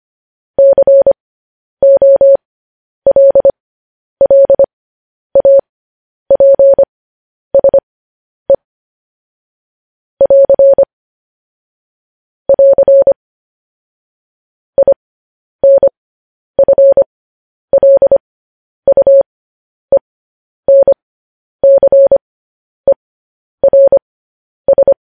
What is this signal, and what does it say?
Morse code, heard as a keyed tone, about 25 wpm, spaced as for 10 wpm, 550 Hz. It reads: COLLAPSE <AR> <AR> INFLUENCERS
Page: news headlines in morse code at 10 wpm